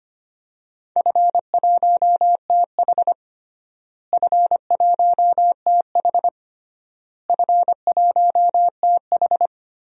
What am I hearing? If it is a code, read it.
Morse code, heard as a keyed tone, about 25 wpm, 700 Hz. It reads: F1TH F1TH F1TH
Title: DXCC-100 Call Signs - 4 Characters - Repeated 3x - 25wpm